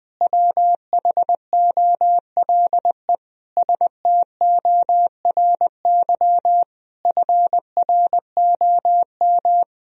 Morse code, heard as a keyed tone, about 20 wpm, 700 Hz. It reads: WHOLE STORY FROM
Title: Sentences from Top 1000 Words - Review Part 4 - 20wpm